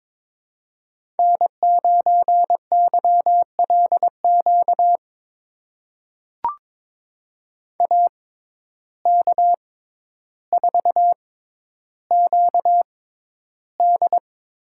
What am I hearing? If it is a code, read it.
Morse code, heard as a keyed tone, about 22 wpm, spaced as for 11 wpm, 700 Hz. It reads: N9YLQ AK4QD
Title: USA Call Signs - 5 Characters - 6x Character Spacing - 22wpm